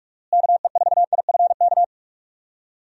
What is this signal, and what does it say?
Morse code, heard as a keyed tone, about 47 wpm, 700 Hz. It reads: KE4IFK
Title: USA Call Signs - 6 Characters - Repeated 3x - 45wpm